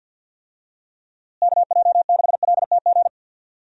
Morse code, heard as a keyed tone, about 48 wpm, 700 Hz. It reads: KJ6LTG
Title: USA Call Signs - 6 Characters - Repeated 2x - 50wpm